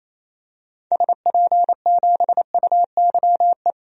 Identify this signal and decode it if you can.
Morse code, heard as a keyed tone, about 27 wpm, 700 Hz. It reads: SP7UYE
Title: DXCC-100 Call Signs - 6 Characters - 28wpm